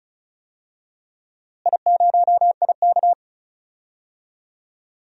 Morse code, heard as a keyed tone, about 34 wpm, 700 Hz. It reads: I0IK
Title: DXCC-100 Call Signs - 4 Characters - Repeated 3x - 35wpm